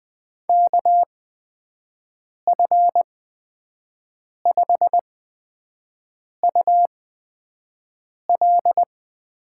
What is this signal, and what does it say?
Morse code, heard as a keyed tone, about 20 wpm, spaced as for 5 wpm, 700 Hz. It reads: KF5UL
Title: USA Call Signs - 6 Characters - 8x Character Spacing - 20wpm